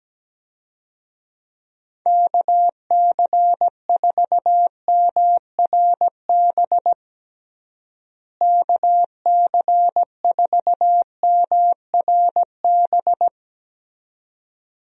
Morse code, heard as a keyed tone, about 17 wpm, 700 Hz. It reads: KC4MRB KC4MRB
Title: USA Call Signs - 6 Characters - Repeated 2x - 17wpm